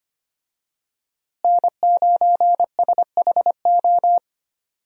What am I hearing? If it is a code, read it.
Morse code, heard as a keyed tone, about 25 wpm, 700 Hz. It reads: N9SHO
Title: USA Call Signs - 5 Characters - Repeated 2x - 25wpm